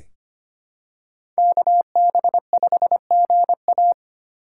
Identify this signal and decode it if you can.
Morse code, heard as a keyed tone, about 25 wpm, 700 Hz. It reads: KB5GA
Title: K1USN SST QSO Practice - 25wpm